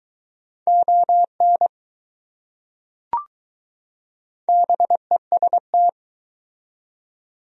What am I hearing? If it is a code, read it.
Morse code, heard as a keyed tone, about 20 wpm, 700 Hz. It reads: ON BEST